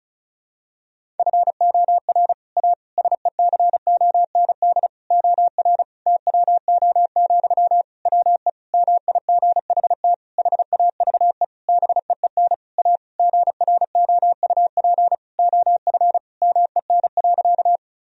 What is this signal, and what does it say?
Morse code, heard as a keyed tone, about 36 wpm, 700 Hz. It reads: FOR A SECOND OR TWO, WE MIGHT HAVE BEEN A GROUP OF MEN.
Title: Sentences from Top 1000 Words - Review Part 1 - 35wpm